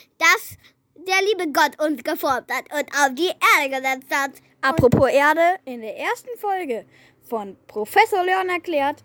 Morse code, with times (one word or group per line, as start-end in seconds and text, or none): none